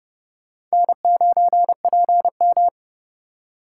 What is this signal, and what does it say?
Morse code, heard as a keyed tone, about 36 wpm, 700 Hz. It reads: N9PM
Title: Speed Racing - USA 1x2 Call Signs 30wpm